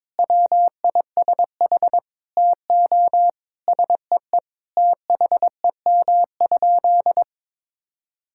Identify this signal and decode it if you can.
Morse code, heard as a keyed tone, about 22 wpm, 700 Hz. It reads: WISH TO SEE THEM?
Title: Sentences from Top 1600 Words 22wpm